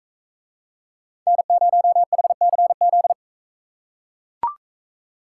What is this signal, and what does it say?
Morse code, heard as a keyed tone, about 40 wpm, 700 Hz. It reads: N0HCZ